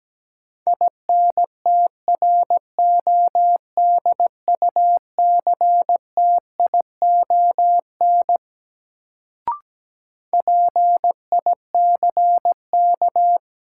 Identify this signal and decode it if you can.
Morse code, heard as a keyed tone, about 17 wpm, 700 Hz. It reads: INTRODUCTION PICK
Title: Sentences from Top 1000 Words - New Word Drill - 17wpm